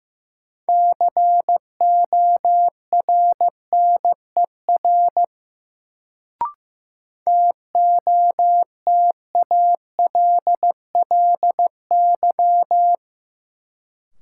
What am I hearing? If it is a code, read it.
Morse code, heard as a keyed tone, about 15 wpm, 700 Hz. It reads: CORNER TOTALLY
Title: Sentences from Top 1600 Words - New Word Drill Alt - 15wpm